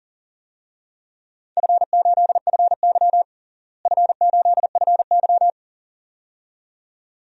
Morse code, heard as a keyed tone, about 40 wpm, 700 Hz. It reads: F8FY F8FY
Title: DXCC-100 Call Signs - 4 Characters - Repeated 2x - 40wpm